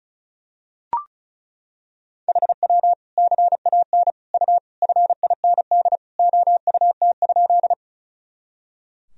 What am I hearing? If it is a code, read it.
Morse code, heard as a keyed tone, about 34 wpm, 700 Hz. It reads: HW CAN U FIND OUT?
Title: Sentences from Top 200 Words and Abbreviations 35wpm